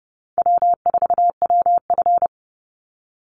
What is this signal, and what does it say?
Morse code, heard as a keyed tone, about 30 wpm, 700 Hz. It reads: W4WF
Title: Call Signs - 30wpm